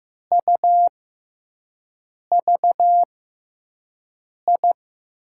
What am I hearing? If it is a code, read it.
Morse code, heard as a keyed tone, about 15 wpm, spaced as for 5 wpm, 700 Hz. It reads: UVI
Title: USA Call Signs - 6 Characters - 6x Character Spacing - 15wpm